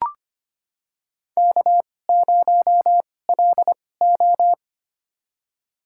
Morse code, heard as a keyed tone, about 25 wpm, spaced as for 18 wpm, 700 Hz. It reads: K0LO
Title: USA Call Signs - 2x Character Spacing - 25wpm